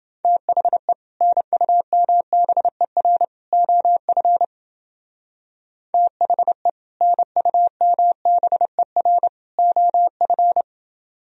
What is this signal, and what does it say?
Morse code, heard as a keyed tone, about 28 wpm, 700 Hz. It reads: THE NUMBER OF THE NUMBER OF